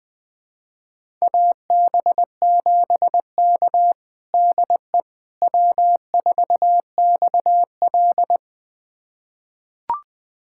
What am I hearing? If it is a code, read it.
Morse code, heard as a keyed tone, about 21 wpm, 700 Hz. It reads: AB7K DE W4XL
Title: USA Callsign DE Callsign - 4 Characters - 20wpm